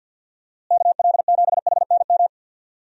Morse code, heard as a keyed tone, about 52 wpm, 700 Hz. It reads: KL7HNM
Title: USA Call Signs - 6 Characters - Repeated 3x - 50wpm